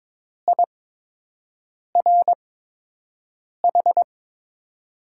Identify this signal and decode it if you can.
Morse code, heard as a keyed tone, about 22 wpm, spaced as for 6 wpm, 700 Hz. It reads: IRH
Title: USA Call Signs - 8x Character Spacing - 22wpm